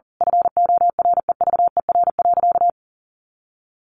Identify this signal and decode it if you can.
Morse code, heard as a keyed tone, about 40 wpm, 700 Hz. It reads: FOREVER.
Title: A Sound Of Thunder 40wpm